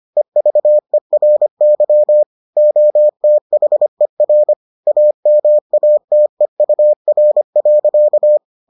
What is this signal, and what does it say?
Morse code, heard as a keyed tone, about 25 wpm, 600 Hz. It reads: EVERY OTHER AMATEUR.